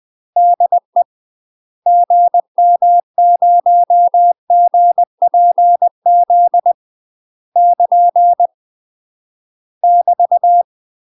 Morse code, heard as a keyed tone, about 20 wpm, 700 Hz. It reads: DE GM0GPZ <KN> <BT>
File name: UK MASTER QSOs TRAINING / 20WPM / UK MASTER QSOs 0019